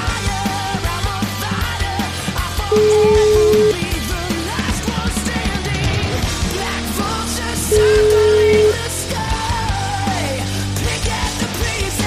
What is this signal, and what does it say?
Ringback tone (Europe, 425 Hz), with noise.